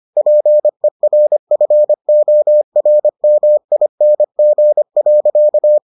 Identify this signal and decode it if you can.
Morse code, heard as a keyed tone, about 25 wpm, 600 Hz. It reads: PERFORMING.